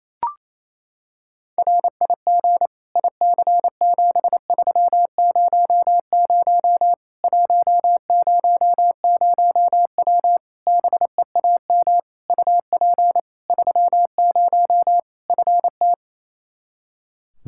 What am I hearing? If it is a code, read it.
Morse code, heard as a keyed tone, about 28 wpm, 700 Hz. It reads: RIG IC7300 100W BEAM UP 30 FT
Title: Practice QSO - Exchange 2 - Info - 28wpm